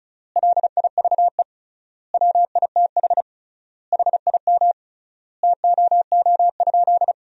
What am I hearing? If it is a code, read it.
Morse code, heard as a keyed tone, about 34 wpm, 700 Hz. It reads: LIVE WITH HIM TOO?
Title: 3x Word-Spacing - Sentences from Top 500 Words - 35wpm